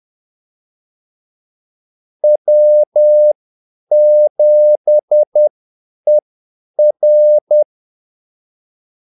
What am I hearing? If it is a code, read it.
Morse code, heard as a keyed tone, about 10 wpm, spaced as for 8 wpm, 600 Hz. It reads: W7ER
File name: Callsigns-10-8wpm